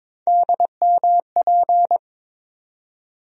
Morse code, heard as a keyed tone, about 22 wpm, 700 Hz. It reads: DMP
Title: USA Call Sign Postfixes - 3 Letters 22wpm